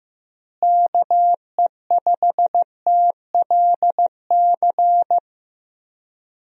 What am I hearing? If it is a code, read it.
Morse code, heard as a keyed tone, about 15 wpm, 700 Hz. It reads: KE5TLC
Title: USA Call Signs - 6 Characters - Repeated 3x - 15wpm